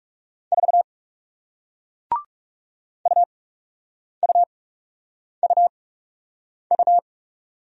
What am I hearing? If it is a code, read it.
Morse code, heard as a keyed tone, about 39 wpm, spaced as for 8 wpm, 700 Hz. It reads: 4 UUUU